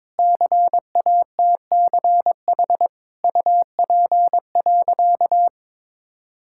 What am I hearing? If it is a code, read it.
Morse code, heard as a keyed tone, about 22 wpm, 700 Hz. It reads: CATCH UP.
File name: Difficult Sentences with Hints - 22wpm